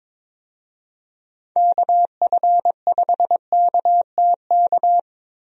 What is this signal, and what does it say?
Morse code, heard as a keyed tone, about 22 wpm, 700 Hz. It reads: KF5KTK
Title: USA Call Signs - 6 Characters - Repeated 3x - 22wpm